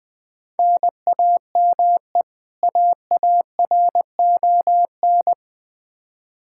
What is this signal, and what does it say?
Morse code, heard as a keyed tone, about 20 wpm, 700 Hz. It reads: NAME AARON
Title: Practice QSO - Exchange 1 - Beginning - 20wpm